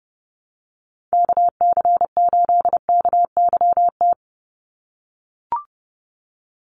Morse code, heard as a keyed tone, about 30 wpm, 700 Hz. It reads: KC8KYT
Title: USA Call Signs 30wpm